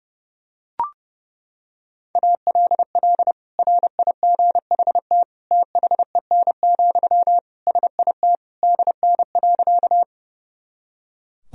Morse code, heard as a keyed tone, about 30 wpm, 700 Hz. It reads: ALL RIGHT THEN, SIT DN.
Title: Sentences from Top 500 Words and Abbreviations 30wpm